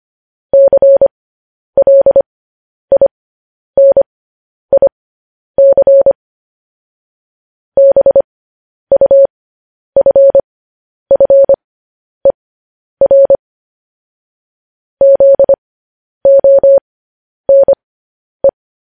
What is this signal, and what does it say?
Morse code, heard as a keyed tone, about 25 wpm, spaced as for 10 wpm, 550 Hz. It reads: CLINIC BUFFER ZONE
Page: news headlines in morse code at 10 wpm